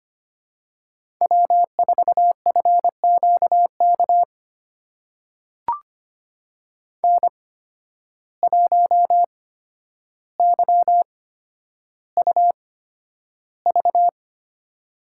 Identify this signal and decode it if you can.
Morse code, heard as a keyed tone, about 25 wpm, spaced as for 11 wpm, 700 Hz. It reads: W4FQK N1YUV
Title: USA Call Signs - 5 Characters - 8x Character Spacing - 25wpm